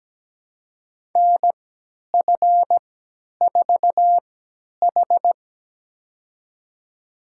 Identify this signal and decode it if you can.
Morse code, heard as a keyed tone, about 17 wpm, spaced as for 10 wpm, 700 Hz. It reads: NF4H
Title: USA Call Signs - 3x Character Spacing - 17wpm